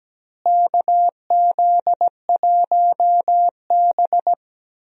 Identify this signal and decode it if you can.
Morse code, heard as a keyed tone, about 17 wpm, 700 Hz. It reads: KZ1B
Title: USA Call Signs - 4 Characters - Repeated 3x - 17wpm